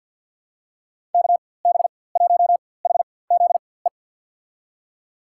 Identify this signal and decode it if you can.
Morse code, heard as a keyed tone, about 49 wpm, spaced as for 23 wpm, 700 Hz. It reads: KB1HZE
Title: USA Call Signs - 6 Characters - 4x Character Spacing - 50wpm